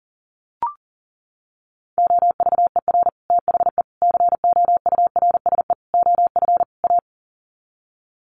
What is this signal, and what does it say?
Morse code, heard as a keyed tone, about 40 wpm, 700 Hz. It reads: OVER THE COURSE OF A
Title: Sets of 5 Words Encore 40wpm